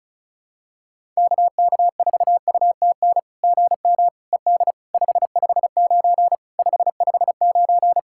Morse code, heard as a keyed tone, about 36 wpm, 700 Hz. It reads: KK4UTN GM ED 559 559
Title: SOTA QSO Practice 35wpm